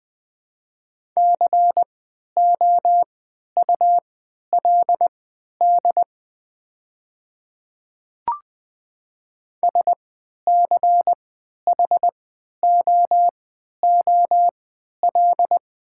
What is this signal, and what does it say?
Morse code, heard as a keyed tone, about 20 wpm, spaced as for 11 wpm, 700 Hz. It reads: COULD SCHOOL
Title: 3x Character Spacing - Top 300 Words - 20wpm